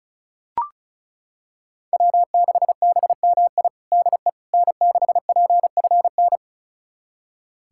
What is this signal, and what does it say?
Morse code, heard as a keyed tone, about 36 wpm, 700 Hz. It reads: W6BMI DE N6PFN
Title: USA Callsign DE Callsign - 5 Characters - 35wpm